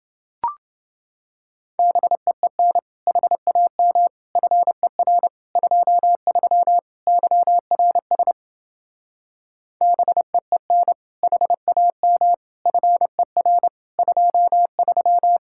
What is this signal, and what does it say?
Morse code, heard as a keyed tone, about 29 wpm, 700 Hz. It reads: BEEN HAM FER 23 YRS BEEN HAM FER 23